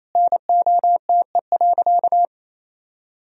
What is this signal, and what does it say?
Morse code, heard as a keyed tone, about 28 wpm, 700 Hz. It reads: NOTE.
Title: Sentences from Top 500 Words and Abbreviations 28wpm